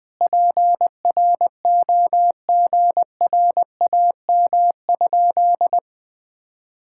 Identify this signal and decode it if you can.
Morse code, heard as a keyed tone, about 20 wpm, 700 Hz. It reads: PROGRAM?
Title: Sentences from Top 1000 Words - Review Part 5 - 20wpm